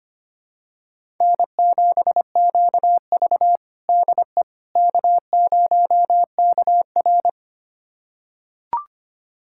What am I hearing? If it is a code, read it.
Morse code, heard as a keyed tone, about 25 wpm, 700 Hz. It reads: N7QV DE K0KR
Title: USA Callsign DE Callsign - 4 Characters - 25wpm